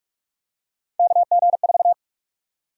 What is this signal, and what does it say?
Morse code, heard as a keyed tone, about 48 wpm, 700 Hz. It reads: KG4
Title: USA Call Sign Prefixes - Repeated 2x - 45wpm